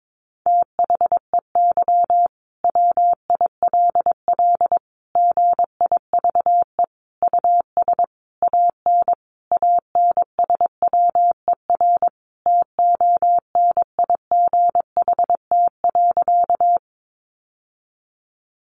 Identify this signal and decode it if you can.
Morse code, heard as a keyed tone, about 22 wpm, 700 Hz. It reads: THEY WILL GIVE US AN ANSWER TONIGHT.
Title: Sentences from Top 600 Words 22wpm